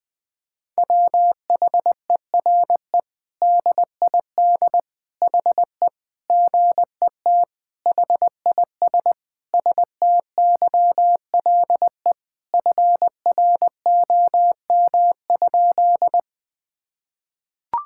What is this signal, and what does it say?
Morse code, heard as a keyed tone, about 20 wpm, 700 Hz. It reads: WHERE DID HE GET HIS STYLE FROM?
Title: Sentences from Top 1000 Words - Review Part 5 - 20wpm